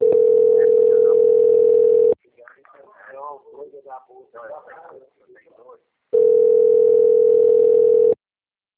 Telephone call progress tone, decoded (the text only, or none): ringback tone